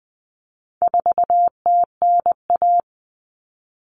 Morse code, heard as a keyed tone, about 20 wpm, 700 Hz. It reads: 4TNA
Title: Common Cut Numbers - 4-Digit 20wpm